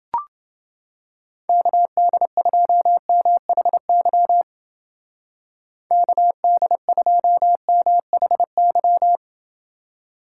Morse code, heard as a keyed tone, about 28 wpm, 700 Hz. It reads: KD2MHY KD2MHY